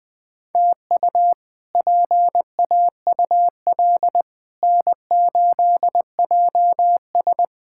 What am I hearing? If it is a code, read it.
Morse code, heard as a keyed tone, about 21 wpm, 700 Hz. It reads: TU PAUL N8JS
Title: K1USN SST QSO Practice - 20wpm